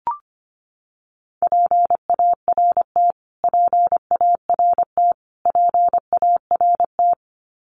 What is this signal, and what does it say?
Morse code, heard as a keyed tone, about 25 wpm, 700 Hz. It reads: PART PART PART
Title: Top 300 Words Repeated 25wpm